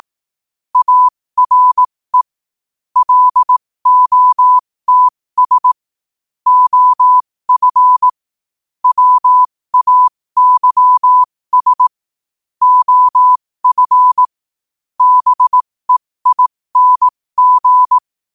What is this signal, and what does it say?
Morse code, heard as a keyed tone, about 18 wpm, spaced as for 15 wpm, 1000 Hz. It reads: ARE LOTS OF WAYS OF BEING
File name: qotdfiles / 15wpm